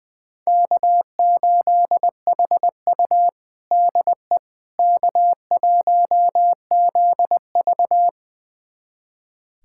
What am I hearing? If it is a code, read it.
Morse code, heard as a keyed tone, about 20 wpm, 700 Hz. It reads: K8HU DE K1ZV
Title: USA Callsign DE Callsign - 4 Characters - 20wpm